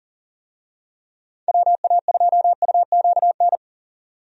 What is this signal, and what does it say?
Morse code, heard as a keyed tone, about 41 wpm, 700 Hz. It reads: WA2UQN
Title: USA Call Signs - 6 Characters - 40wpm